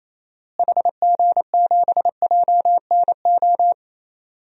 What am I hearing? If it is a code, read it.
Morse code, heard as a keyed tone, about 28 wpm, 700 Hz. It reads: HG7JNO
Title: DXCC-100 Call Signs - 6 Characters - Repeated 2x - 28wpm